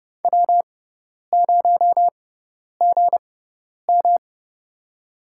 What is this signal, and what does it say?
Morse code, heard as a keyed tone, about 29 wpm, spaced as for 10 wpm, 700 Hz. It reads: W0GM
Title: USA Call Signs - 4 Characters - 6x Character Spacing - 30wpm